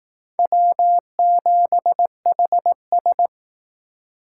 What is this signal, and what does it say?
Morse code, heard as a keyed tone, about 18 wpm, 700 Hz. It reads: W7HS
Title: Speed Racing - USA 1x2 Call Signs 15wpm